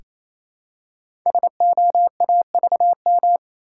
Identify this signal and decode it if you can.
Morse code, heard as a keyed tone, about 27 wpm, 700 Hz. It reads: SOAVM
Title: Code Groups - 5 Letters - 28wpm